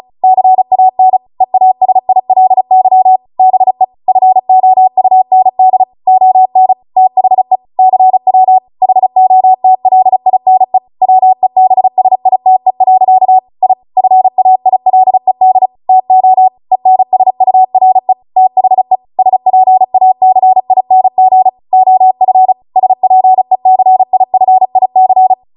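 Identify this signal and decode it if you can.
Morse code, heard as a keyed tone, about 36 wpm, 750 Hz. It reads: CAN EASILY BE FOUND ON THE CW HOTLINE WEBSITE. I FAILED TO ENSURE THE SPACING OF SPECIFIC
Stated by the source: Amateur radio Morse code practice archive transmission at 35 wpm.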